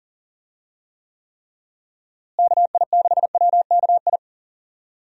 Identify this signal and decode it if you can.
Morse code, heard as a keyed tone, about 40 wpm, 700 Hz. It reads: KI6WKI
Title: USA Call Signs - 6 Characters - Repeated 3x - 40wpm